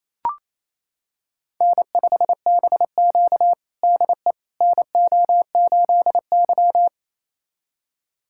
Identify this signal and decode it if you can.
Morse code, heard as a keyed tone, about 28 wpm, 700 Hz. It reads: N5BQ DE NO8Y
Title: USA Callsign DE Callsign - 4 Characters - 28wpm